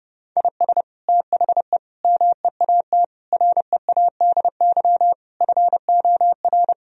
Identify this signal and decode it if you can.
Morse code, heard as a keyed tone, about 30 wpm, 700 Hz. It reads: IS THE MEAT READY FOR
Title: Sentences from Top 1000 Words - Review Part 3 - 30wpm